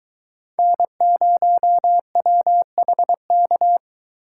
Morse code, heard as a keyed tone, about 22 wpm, 700 Hz. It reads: N0WHK